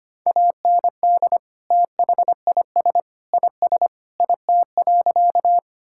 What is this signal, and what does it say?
Morse code, heard as a keyed tone, about 25 wpm, 700 Hz. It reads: AND THIS IS IT.
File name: practice / 25wpm / Sentences with 3-Word Phrases - 25wpm